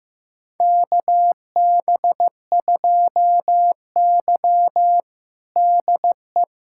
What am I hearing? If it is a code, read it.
Morse code, heard as a keyed tone, about 15 wpm, 700 Hz. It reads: KB2Y DE
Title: USA Callsign DE Callsign - 4 Characters - 15wpm